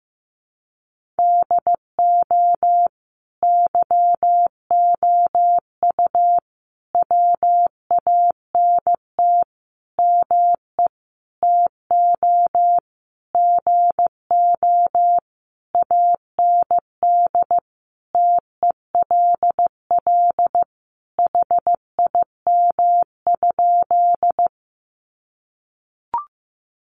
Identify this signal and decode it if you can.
Morse code, heard as a keyed tone, about 15 wpm, 700 Hz. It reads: DO YOU WANT ME TO GO AND TELL HIM?
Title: Sentences from Top 200 Words 15wpm